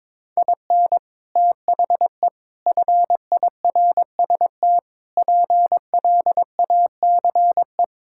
Morse code, heard as a keyed tone, about 21 wpm, 700 Hz. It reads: IN THE FIRST PLACE
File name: practice / 22wpm / Sentences with 4-Word Phrases - 22wpm